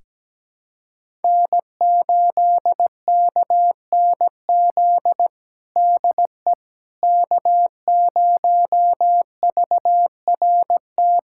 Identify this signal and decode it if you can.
Morse code, heard as a keyed tone, about 17 wpm, 700 Hz. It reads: N8KNZ DE K0VRT
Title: USA Callsign DE Callsign - 5 Characters - 17wpm